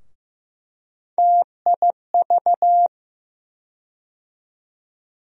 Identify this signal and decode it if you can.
Morse code, heard as a keyed tone, about 15 wpm, 700 Hz. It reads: TIV
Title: USA Call Sign Postfixes - 3 Letters 15wpm